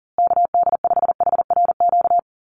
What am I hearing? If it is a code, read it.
Morse code, heard as a keyed tone, about 40 wpm, 700 Hz. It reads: KD5HRQ
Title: USA Call Signs Repeated 3x 40wpm